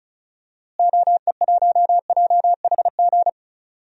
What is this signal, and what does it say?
Morse code, heard as a keyed tone, about 36 wpm, 700 Hz. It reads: OE1JHG
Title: DXCC-100 Call Signs - 6 Characters - Repeated 3x - 35wpm